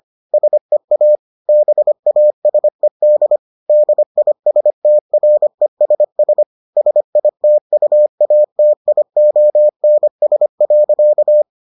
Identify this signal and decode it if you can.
Morse code, heard as a keyed tone, about 25 wpm, 600 Hz. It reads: SEA BASED DISTRESS SITUATIONS.